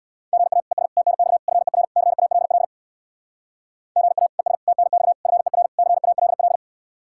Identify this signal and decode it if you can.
Morse code, heard as a keyed tone, about 62 wpm, 700 Hz. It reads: GA ES TNX FER CALL GA ES TNX FER CALL
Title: Speed-Racing Common QSO Phrases - 50wpm